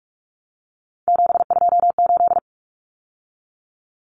Morse code, heard as a keyed tone, about 44 wpm, 700 Hz. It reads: 728
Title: Common Cut Numbers - 3-Digit 45wpm